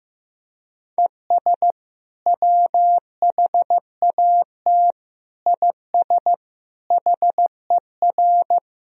Morse code, heard as a keyed tone, about 16 wpm, 700 Hz. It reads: ES WHAT IS HER
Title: Sentences from Top 200 Words and Abbreviations 15wpm